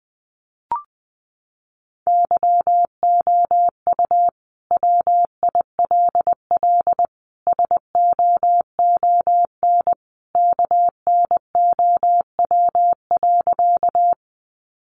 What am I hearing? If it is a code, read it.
Morse code, heard as a keyed tone, about 20 wpm, 700 Hz. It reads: YOU WILL SOON KNOW.
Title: Sentences from Top 400 Words 20wpm